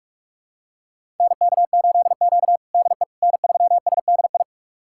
Morse code, heard as a keyed tone, about 44 wpm, 700 Hz. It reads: NK8Q DE N3SDI
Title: USA Callsign DE Callsign 45wpm